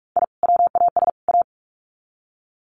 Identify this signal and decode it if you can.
Morse code, heard as a keyed tone, about 44 wpm, 700 Hz. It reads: I WAS A
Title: Sets of 3 Words 45wpm